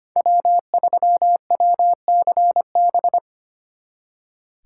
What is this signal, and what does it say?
Morse code, heard as a keyed tone, about 31 wpm, 700 Hz. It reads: W3WCB
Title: Speed Racing - USA 1x3 Call Signs 25wpm